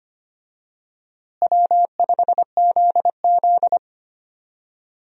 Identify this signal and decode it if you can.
Morse code, heard as a keyed tone, about 25 wpm, 700 Hz. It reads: W5ZZ
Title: USA Call Signs - 4 Characters - Repeated 3x - 25wpm